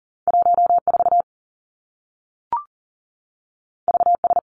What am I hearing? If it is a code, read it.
Morse code, heard as a keyed tone, about 39 wpm, 700 Hz. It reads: 1 4 V S
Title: Two Challenging Letter-Number Combinations 40wpm